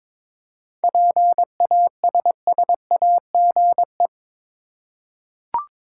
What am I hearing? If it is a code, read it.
Morse code, heard as a keyed tone, about 22 wpm, 700 Hz. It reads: PASSAGE
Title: Seven Letter Words - 22wpm